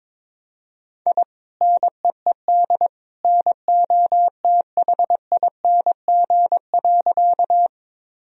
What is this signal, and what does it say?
Morse code, heard as a keyed tone, about 22 wpm, 700 Hz. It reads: I NEED NOTHING.
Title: Sentences from Top 1000 Words - Review Part 1 - 22wpm